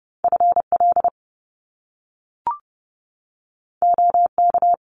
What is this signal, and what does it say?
Morse code, heard as a keyed tone, about 30 wpm, 700 Hz. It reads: FL OK
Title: US States 30wpm